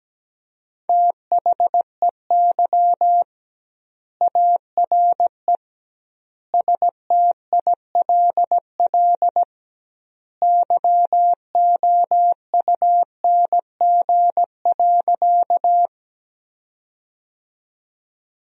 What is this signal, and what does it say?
Morse code, heard as a keyed tone, about 17 wpm, 700 Hz. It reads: THEY ARE STILL YOUNG.
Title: 2x Word-Spacing - Sentences from Top 400 Words - 17wpm